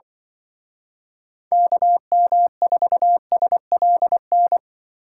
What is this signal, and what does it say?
Morse code, heard as a keyed tone, about 24 wpm, 700 Hz. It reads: KM4SLN
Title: Speed Racing - USA 2x3 Call Signs 20wpm